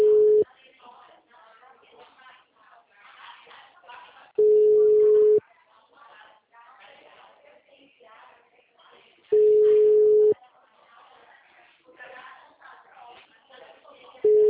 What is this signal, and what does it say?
Ringback tone (Europe, 425 Hz), with speech.